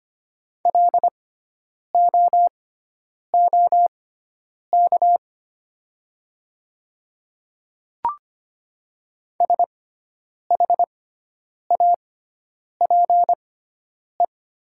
Morse code, heard as a keyed tone, about 25 wpm, spaced as for 9 wpm, 700 Hz. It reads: LOOK SHAPE
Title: 6x Character Spacing - Top 500 Words - 25wpm